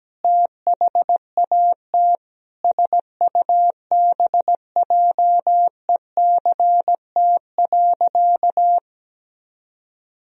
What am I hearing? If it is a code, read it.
Morse code, heard as a keyed tone, about 17 wpm, 700 Hz. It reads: THAT SUBJECT.
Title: Sentences from Top 1000 Words - Review Part 1 - 17wpm